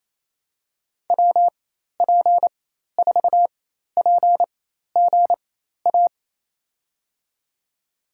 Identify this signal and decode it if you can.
Morse code, heard as a keyed tone, about 28 wpm, spaced as for 13 wpm, 700 Hz. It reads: WP4PGA
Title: USA Call Signs - 6 Characters - 4x Character Spacing - 28wpm